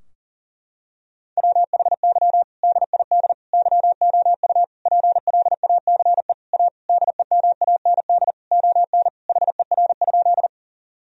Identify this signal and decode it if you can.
Morse code, heard as a keyed tone, about 37 wpm, 700 Hz. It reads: WHY DID YOU PLACE A DEMAND ON HER?
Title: Sentences from Top 1000 Words - Review Part 4 - 40wpm